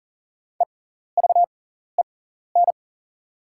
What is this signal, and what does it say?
Morse code, heard as a keyed tone, about 40 wpm, spaced as for 14 wpm, 700 Hz. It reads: EVEN
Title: 6x Character Spacing - Top 500 Words - 40wpm